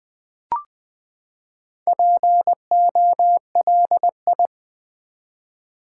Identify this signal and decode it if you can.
Morse code, heard as a keyed tone, about 20 wpm, 700 Hz. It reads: POLI
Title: Build A Word - 4 Letters At A Time - 10-Letter Words - 20wpm